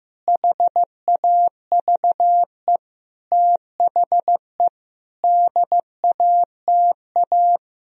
Morse code, heard as a keyed tone, about 15 wpm, 700 Hz. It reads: HAVE THE DATA
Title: Sentences from Top 1000 Words - Review Part 3 - 15wpm